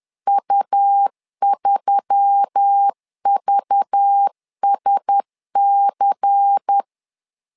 Morse code, heard as a keyed tone, about 10 wpm, 800 Hz. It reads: U3VSC
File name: LEZIONI / 5 LBVC / 5 LBVC 10wpm 2349